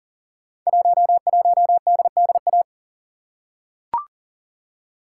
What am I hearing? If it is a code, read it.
Morse code, heard as a keyed tone, about 38 wpm, 700 Hz. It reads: 11DDA
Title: Code Groups - 5 Letter:Numbers - 40wpm